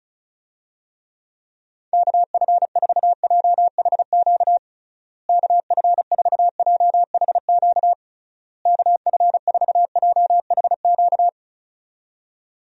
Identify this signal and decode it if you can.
Morse code, heard as a keyed tone, about 34 wpm, 700 Hz. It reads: KF4JHQ KF4JHQ KF4JHQ
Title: USA Call Signs - 6 Characters - Repeated 3x - 35wpm